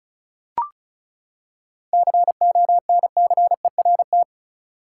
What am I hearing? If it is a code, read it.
Morse code, heard as a keyed tone, about 36 wpm, 700 Hz. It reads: CONCERT
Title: Seven Letter Words - 35wpm